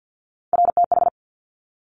Morse code, heard as a keyed tone, about 50 wpm, 700 Hz. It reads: RTH
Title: Top Three Letter Combinations in Words 50wpm